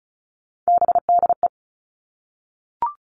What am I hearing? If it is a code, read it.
Morse code, heard as a keyed tone, about 36 wpm, 700 Hz. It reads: BDE
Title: Three Challenging Letter-Number Combinations 35wpm